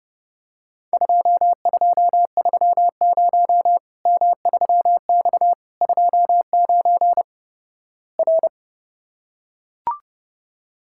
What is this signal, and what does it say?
Morse code, heard as a keyed tone, about 30 wpm, 700 Hz. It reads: 2230 M3X 29 R
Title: Worked All Europe DX Contest - QTC Exchange - 30wpm